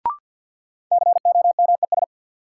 Morse code, heard as a keyed tone, about 52 wpm, 700 Hz. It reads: COMES
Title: Sentences from Top 1000 Words - Review Part 5 - New Word Drill - 50wpm